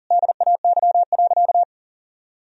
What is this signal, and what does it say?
Morse code, heard as a keyed tone, about 37 wpm, 700 Hz. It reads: DAY.
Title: Sentences from Top 1000 Words - Review Part 4 - 40wpm